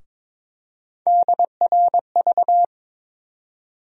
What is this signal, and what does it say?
Morse code, heard as a keyed tone, about 22 wpm, 700 Hz. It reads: DRV
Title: USA Call Sign Postfixes - 3 Letters 22wpm